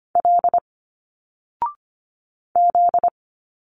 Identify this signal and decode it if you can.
Morse code, heard as a keyed tone, about 25 wpm, 700 Hz. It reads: L Z
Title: Single Letters 25wpm